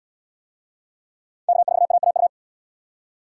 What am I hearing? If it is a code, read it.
Morse code, heard as a keyed tone, about 77 wpm, 700 Hz. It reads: K4AAU